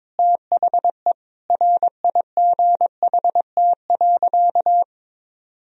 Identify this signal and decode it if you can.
Morse code, heard as a keyed tone, about 21 wpm, 700 Hz. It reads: THE RIGHT.
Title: Sentences from Top 1000 Words - Review Part 1 - 22wpm